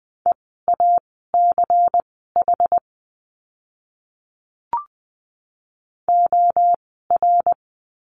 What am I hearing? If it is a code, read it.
Morse code, heard as a keyed tone, about 20 wpm, spaced as for 14 wpm, 700 Hz. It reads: EACH OR
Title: 192 Lesson 31f - Letter v - Words 20wpm